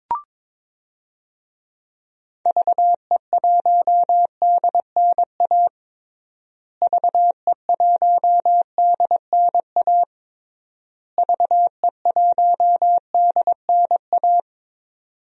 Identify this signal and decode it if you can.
Morse code, heard as a keyed tone, about 21 wpm, 700 Hz. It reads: VE1DNA VE1DNA VE1DNA
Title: DXCC-100 Call Signs - 6 Characters - Repeated 3x - 22wpm